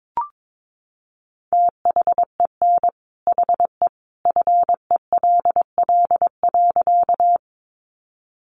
Morse code, heard as a keyed tone, about 22 wpm, 700 Hz. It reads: THEN HE FELL.
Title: A Sound Of Thunder 22wpm